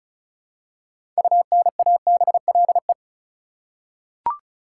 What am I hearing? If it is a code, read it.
Morse code, heard as a keyed tone, about 34 wpm, 700 Hz. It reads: UNABLE
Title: Six Letter Words - 35wpm